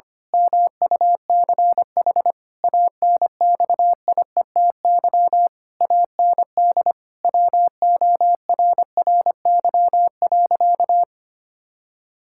Morse code, heard as a keyed tone, about 25 wpm, 700 Hz. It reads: MUCH ANXIETY AND WORRY.
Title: Pangram Challenge v2 - 25wpm